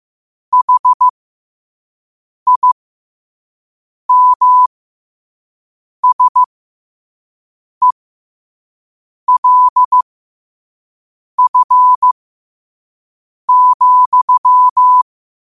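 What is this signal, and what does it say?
Morse code, heard as a keyed tone, about 15 wpm, spaced as for 5 wpm, 1000 Hz. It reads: HIMSELF,
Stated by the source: Morse code quote of the day sent at 5 wpm.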